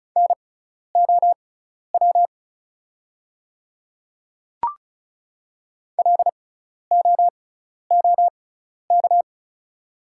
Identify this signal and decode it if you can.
Morse code, heard as a keyed tone, about 36 wpm, spaced as for 12 wpm, 700 Hz. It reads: NOW LOOK
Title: 6x Character Spacing - Top 200 Words - 35wpm